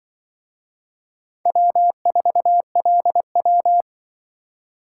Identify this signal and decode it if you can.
Morse code, heard as a keyed tone, about 23 wpm, 700 Hz. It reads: W4LW
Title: Speed Racing - USA 1x2 Call Signs 20wpm